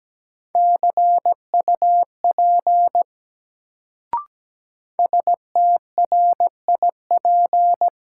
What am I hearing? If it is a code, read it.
Morse code, heard as a keyed tone, about 17 wpm, 700 Hz. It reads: CUP STRIP
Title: Sentences from Top 1600 Words - New Word Drill - 17wpm